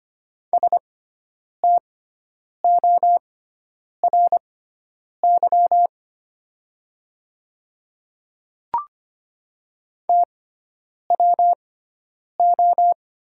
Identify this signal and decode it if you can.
Morse code, heard as a keyed tone, about 25 wpm, spaced as for 9 wpm, 700 Hz. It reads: STORY TWO
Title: 6x Character Spacing - Top 300 Words - 25wpm